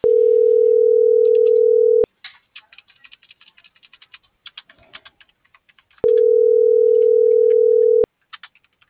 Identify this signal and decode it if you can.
Ringback tone (North America), 440 + 480 Hz.